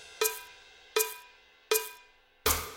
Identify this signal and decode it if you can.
There is no code.